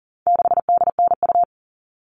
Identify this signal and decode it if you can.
Morse code, heard as a keyed tone, about 40 wpm, 700 Hz. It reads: 6DNU
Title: All Cut Numbers - 4-Digit 40wpm